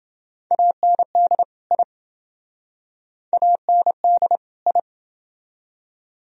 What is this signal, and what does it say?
Morse code, heard as a keyed tone, about 27 wpm, 700 Hz. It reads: AND I AND I